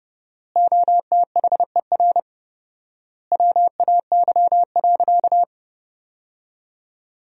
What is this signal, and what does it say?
Morse code, heard as a keyed tone, about 30 wpm, 700 Hz. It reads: OTHER WAY.
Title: 4x Word-Spacing - Sentences from Top 100 Words - 30wpm